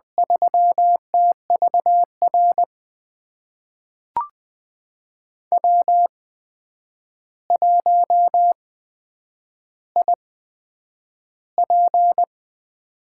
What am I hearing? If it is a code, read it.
Morse code, heard as a keyed tone, about 20 wpm, spaced as for 9 wpm, 700 Hz. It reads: 3TVR W1IP
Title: USA Call Signs - 8x Character Spacing - 20wpm